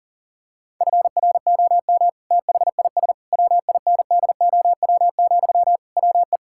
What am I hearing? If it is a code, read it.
Morse code, heard as a keyed tone, about 37 wpm, 700 Hz. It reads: FROM THIS WINDOW, WE
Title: Sentences from Top 1600 Words 40wpm